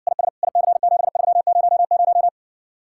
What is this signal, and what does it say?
Morse code, heard as a keyed tone, about 60 wpm, 700 Hz. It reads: IS IC7300